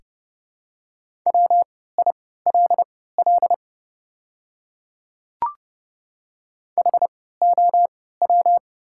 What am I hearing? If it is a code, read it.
Morse code, heard as a keyed tone, about 30 wpm, spaced as for 17 wpm, 700 Hz. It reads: WILL HOW
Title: 3x Character Spacing - Top 100 Words - 30wpm